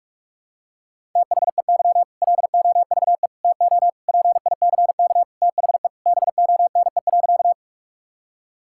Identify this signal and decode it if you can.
Morse code, heard as a keyed tone, about 47 wpm, 700 Hz. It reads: THEY LOVE TO PICK THE BONE.
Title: Sentences from Top 1600 Words 45wpm